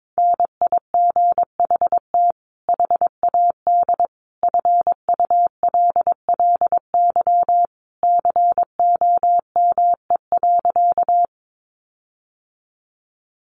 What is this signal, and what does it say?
Morse code, heard as a keyed tone, about 22 wpm, 700 Hz. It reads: NIGHT HAD FULLY COME.
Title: Sentences from Top 1300 Words 22wpm